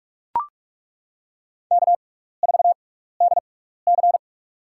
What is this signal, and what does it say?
Morse code, heard as a keyed tone, about 42 wpm, spaced as for 15 wpm, 700 Hz. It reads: K4DC
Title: USA Call Signs - 4 Characters - 6x Character Spacing - 45wpm